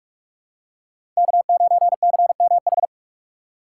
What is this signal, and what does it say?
Morse code, heard as a keyed tone, about 44 wpm, 700 Hz. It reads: K9CMH
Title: USA Call Signs - 5 Characters - 6x Character Spacing - 45wpm